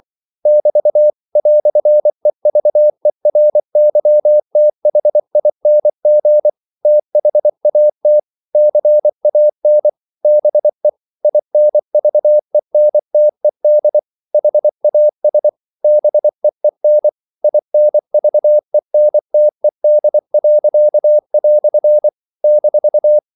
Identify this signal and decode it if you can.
Morse code, heard as a keyed tone, about 24 wpm, 600 Hz. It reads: <BT> "EVERYTHING THAT CAN BE INVENTED HAS BEEN INVENTED." -
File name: CW / Practice / EnglishQuotations0424